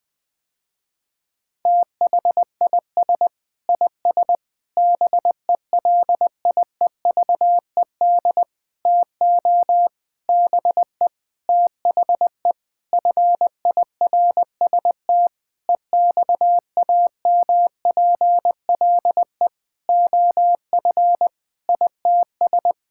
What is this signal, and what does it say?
Morse code, heard as a keyed tone, about 21 wpm, 700 Hz. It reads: THIS IS BELIEVED TO BE THE FIRST EXAMPLE OF ITS
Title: Sentences with 6-Word Phrases - Spoken Beginning - 20wpm